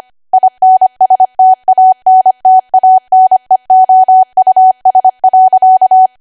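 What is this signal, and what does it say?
Morse code, heard as a keyed tone, about 25 wpm, 750 Hz. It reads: INSTANTANEOUS.
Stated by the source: ARRL 25 WPM code practice archive.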